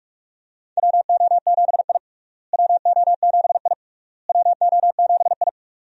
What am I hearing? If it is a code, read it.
Morse code, heard as a keyed tone, about 46 wpm, 700 Hz. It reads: WO7I WO7I WO7I
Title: USA Call Signs - 4 Characters - Repeated 3x - 45wpm